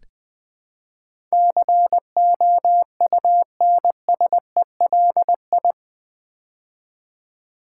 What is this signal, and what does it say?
Morse code, heard as a keyed tone, about 20 wpm, 700 Hz. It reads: COUNSELI
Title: Build A Word - 4 Letters At A Time - 10-Letter Words - 20wpm